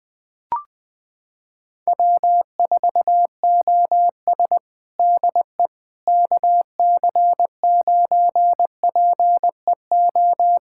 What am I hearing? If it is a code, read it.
Morse code, heard as a keyed tone, about 20 wpm, 690 Hz. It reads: W4OS DE KC9PEO
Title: USA Callsign DE Callsign 20wpm